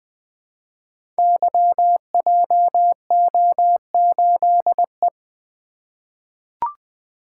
Morse code, heard as a keyed tone, about 20 wpm, 700 Hz. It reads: YJO8E
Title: Code Groups - 5 Letter:Numbers - 20wpm